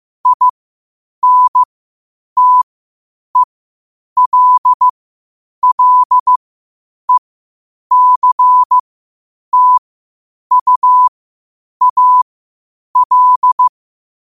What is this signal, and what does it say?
Morse code, heard as a keyed tone, about 15 wpm, spaced as for 8 wpm, 1000 Hz. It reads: INTELLECTUAL
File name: qotdfiles / 8wpm